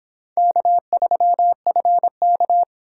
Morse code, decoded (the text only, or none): K3FK